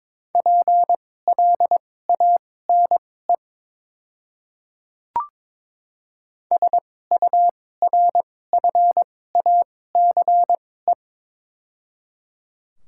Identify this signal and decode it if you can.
Morse code, heard as a keyed tone, about 22 wpm, spaced as for 16 wpm, 700 Hz. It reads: PLANE SURFACE
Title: 2x Character Spacing - Top 500 Words - 22wpm